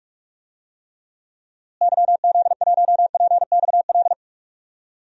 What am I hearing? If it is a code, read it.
Morse code, heard as a keyed tone, about 47 wpm, 700 Hz. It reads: YZ1PXL
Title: DXCC-100 Call Signs - 6 Characters - Repeated 3x - 45wpm